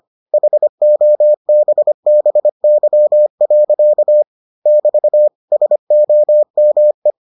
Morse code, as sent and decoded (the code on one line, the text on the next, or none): .... --- -... -... -.-- .-.-.- / -...- / ... --- -- .
HOBBY. <BT> SOME